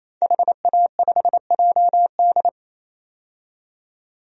Morse code, heard as a keyed tone, about 28 wpm, 700 Hz. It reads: HA5JD
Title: DXCC-100 Call Signs - 5 Characters - 28wpm